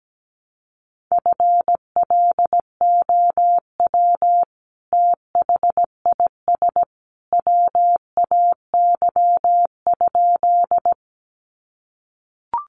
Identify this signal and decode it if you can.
Morse code, heard as a keyed tone, about 17 wpm, 700 Hz. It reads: FLOW THIS WAY?
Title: Sentences from Top 1200 Words 17wpm